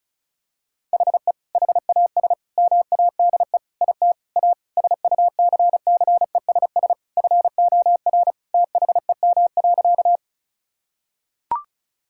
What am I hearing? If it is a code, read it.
Morse code, heard as a keyed tone, about 34 wpm, 700 Hz. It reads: HE HAS MADE IT A SUCCESS FOR THEM.
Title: Sentences from Top 1000 Words - Review Part 6 - 35wpm